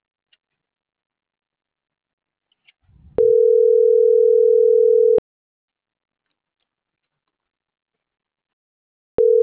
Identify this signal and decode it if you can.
Ringback tone (North America), 440 + 480 Hz.